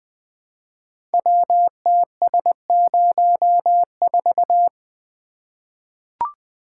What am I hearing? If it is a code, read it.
Morse code, heard as a keyed tone, about 20 wpm, 700 Hz. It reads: WTS04
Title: Code Groups - 5 Letter:Numbers - 20wpm